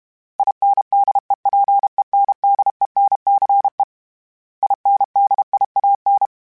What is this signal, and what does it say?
Morse code, heard as a keyed tone, about 32 wpm, 800 Hz. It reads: INDEPENDENCE INDIAN
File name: fromSMs / FromSMs mp3s / Cities@30